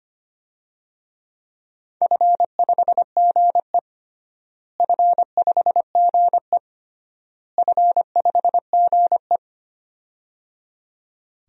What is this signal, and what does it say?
Morse code, heard as a keyed tone, about 25 wpm, 700 Hz. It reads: F5GE F5GE F5GE
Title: DXCC-100 Call Signs - 4 Characters - Repeated 3x - 25wpm